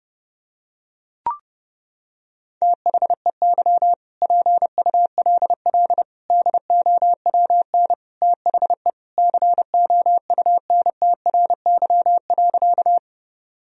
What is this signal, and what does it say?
Morse code, heard as a keyed tone, about 32 wpm, 700 Hz. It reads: THEY PULL DOWN THE COUNTRY.
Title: Sentences from Top 1000 Words - Review Part 2 - 30wpm